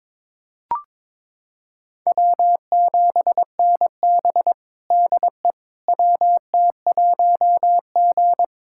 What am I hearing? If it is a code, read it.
Morse code, heard as a keyed tone, about 22 wpm, 700 Hz. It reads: W7NB DE WT1G
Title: USA Callsign DE Callsign - 4 Characters - 22wpm